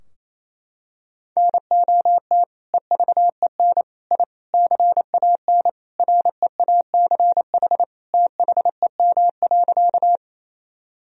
Morse code, heard as a keyed tone, about 28 wpm, 700 Hz. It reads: NOT EVEN I CAN REACH THEM.
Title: Sentences from Top 1000 Words - Review Part 2 - 28wpm